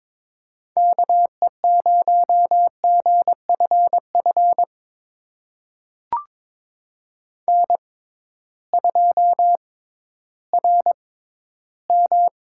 Morse code, heard as a keyed tone, about 22 wpm, spaced as for 13 wpm, 700 Hz. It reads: KE0GFF N2RM
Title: USA Call Signs - 6x Character Spacing - 22wpm